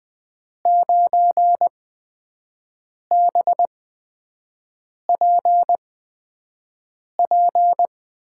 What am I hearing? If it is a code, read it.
Morse code, heard as a keyed tone, about 20 wpm, spaced as for 5 wpm, 700 Hz. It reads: 9BPP
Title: USA Call Signs - 6 Characters - 8x Character Spacing - 20wpm